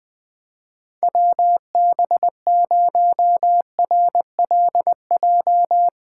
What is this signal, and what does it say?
Morse code, heard as a keyed tone, about 20 wpm, 700 Hz. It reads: WB0RLJ
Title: 2023 POTA Activator Call Signs - 20wpm